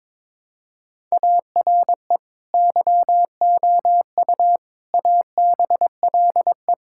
Morse code, heard as a keyed tone, about 22 wpm, 700 Hz. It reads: ARE YOU ABLE
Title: Sentences from Top 1000 Words - Review Part 5 - 22wpm